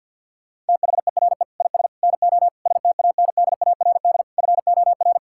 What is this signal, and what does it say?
Morse code, heard as a keyed tone, about 50 wpm, 700 Hz. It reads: THERE IS NO STANDARD FOR
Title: Sentences from Top 1600 Words 50wpm